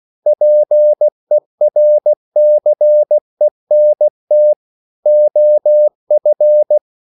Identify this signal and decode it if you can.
Morse code, heard as a keyed tone, about 16 wpm, 600 Hz. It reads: PERCENT OF